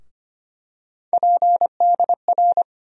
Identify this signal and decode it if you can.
Morse code, heard as a keyed tone, about 25 wpm, 700 Hz. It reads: PDR
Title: USA Call Sign Postfixes - 3 Letters 25wpm